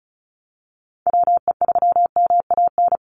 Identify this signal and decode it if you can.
Morse code, heard as a keyed tone, about 34 wpm, 700 Hz. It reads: WE3MAN
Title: USA Call Signs 35wpm